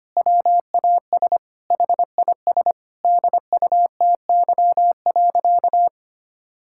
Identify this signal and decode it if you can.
Morse code, heard as a keyed tone, about 25 wpm, 700 Hz. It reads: WAS HIS DUTY.
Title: Sentences from Top 1600 Words 25wpm